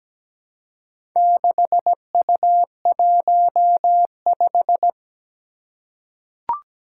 Morse code, heard as a keyed tone, about 17 wpm, 700 Hz. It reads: 6U15
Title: Four Challenging Letter-Number Combinations 17wpm